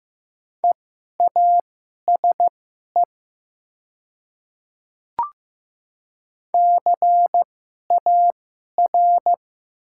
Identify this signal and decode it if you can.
Morse code, heard as a keyed tone, about 15 wpm, spaced as for 11 wpm, 700 Hz. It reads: EASE CAR
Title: 2x Character Spacing - Top 300 Words - 15wpm